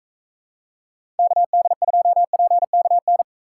Morse code, heard as a keyed tone, about 44 wpm, 700 Hz. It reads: KD2PKN